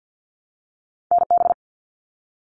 Morse code, heard as a keyed tone, about 50 wpm, 700 Hz. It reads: NB
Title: Canadian Provinces and Territories 50wpm